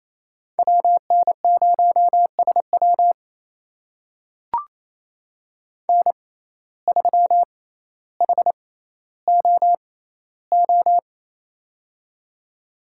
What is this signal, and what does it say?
Morse code, heard as a keyed tone, about 28 wpm, spaced as for 14 wpm, 700 Hz. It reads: WN0SW N3HOO
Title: USA Call Signs - 5 Characters - 6x Character Spacing - 28wpm